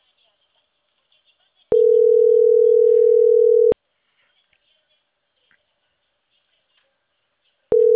Ringback tone (North America), 440 + 480 Hz.